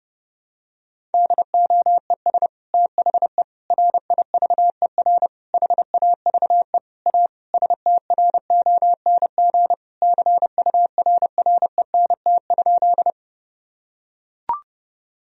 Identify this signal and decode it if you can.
Morse code, heard as a keyed tone, about 30 wpm, 700 Hz. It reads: DOES THE RIVER HAVE A STRONG CURRENT?
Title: Sentences from Top 1000 Words - Review Part 4 - 30wpm